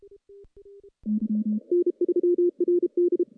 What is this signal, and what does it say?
Morse code, heard as a keyed tone, about 33 wpm, 350 Hz. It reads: N3RD
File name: contests / CW-SS-k5tr- -1300